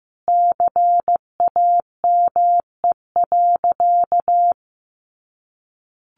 Morse code, heard as a keyed tone, about 15 wpm, 700 Hz. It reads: CAME.
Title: Sentences from Top 1400 Words 15wpm